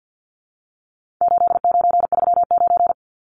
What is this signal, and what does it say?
Morse code, heard as a keyed tone, about 51 wpm, 700 Hz. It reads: 8939
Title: Numbers - 4-Digit 50wpm